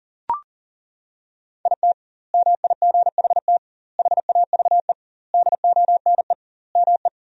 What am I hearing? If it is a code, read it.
Morse code, heard as a keyed tone, about 40 wpm, 700 Hz. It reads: IT MIGHT HAVE DONE ME
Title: 2x Word-Spacing - Sentences from Top 500 Words - 40wpm